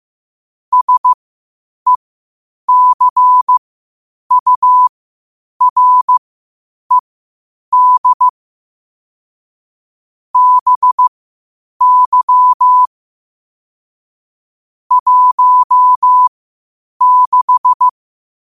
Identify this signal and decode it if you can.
Morse code, heard as a keyed tone, about 15 wpm, spaced as for 8 wpm, 1000 Hz. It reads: SECURED BY 16
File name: qotdfiles / 8wpm